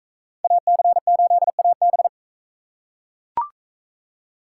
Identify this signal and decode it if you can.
Morse code, heard as a keyed tone, about 41 wpm, 700 Hz. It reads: AC8AB